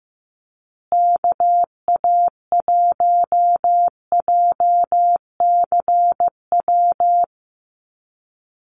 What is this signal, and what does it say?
Morse code, heard as a keyed tone, about 15 wpm, 700 Hz. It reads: KA1JCW
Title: USA Call Signs Repeated 2x 15wpm